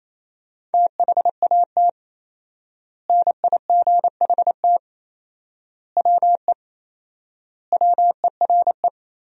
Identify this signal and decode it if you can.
Morse code, heard as a keyed tone, about 28 wpm, 700 Hz. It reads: THAT NIGHT WE WERE
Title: 4x Word-Spacing - Sentences from Top 300 Words - 28wpm